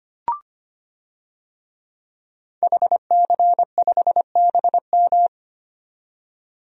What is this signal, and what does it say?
Morse code, heard as a keyed tone, about 25 wpm, 700 Hz. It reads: HC5BM
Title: DXCC-100 Call Signs - 5 Characters - 25wpm